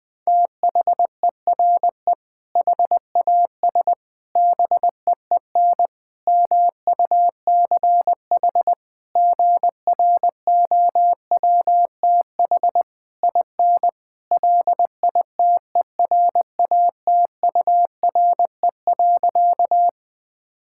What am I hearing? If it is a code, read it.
Morse code, heard as a keyed tone, about 20 wpm, 700 Hz. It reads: THERE HAS BEEN MUCH GROWTH IN LITERATURE.
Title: Sentences from Top 1000 Words - Review Part 5 - 20wpm